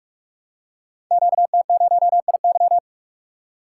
Morse code, heard as a keyed tone, about 44 wpm, 700 Hz. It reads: QT0IY
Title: Code Groups - 5 Letter:Numbers - 45wpm